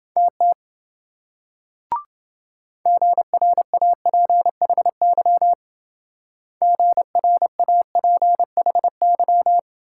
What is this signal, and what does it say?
Morse code, heard as a keyed tone, about 26 wpm, 700 Hz. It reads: TT GRAPHY GRAPHY